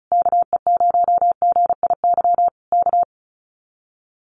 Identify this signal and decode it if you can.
Morse code, heard as a keyed tone, about 34 wpm, 700 Hz. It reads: KE0GIY K
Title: Common QSO Phrases Encore 35wpm